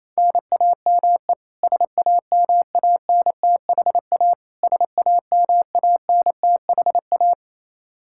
Morse code, heard as a keyed tone, about 28 wpm, 700 Hz. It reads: NAME SAMANTHA SAMANTHA
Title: Practice QSO - Exchange 1 - Beginning - 28wpm